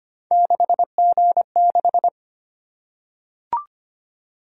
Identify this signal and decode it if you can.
Morse code, heard as a keyed tone, about 25 wpm, 700 Hz. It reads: 6G6
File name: Three Letter-Number Combinations - 25wpm